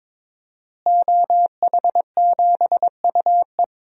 Morse code, heard as a keyed tone, about 22 wpm, 700 Hz. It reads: OH7UE
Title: DXCC-100 Call Signs - 5 Characters - Repeated 3x - 22wpm